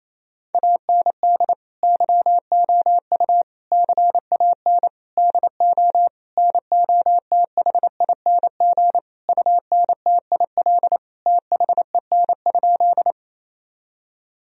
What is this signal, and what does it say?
Morse code, heard as a keyed tone, about 28 wpm, 700 Hz. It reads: AND YOU CAN DO NOTHING UNTIL THEN?
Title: Sentences from Top 1000 Words - Review Part 1 - 28wpm